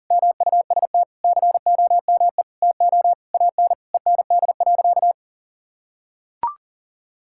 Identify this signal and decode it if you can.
Morse code, heard as a keyed tone, about 40 wpm, 700 Hz. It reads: MUST COME TO AN END.
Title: Sentences from Top 1000 Words - Review Part 5 - 40wpm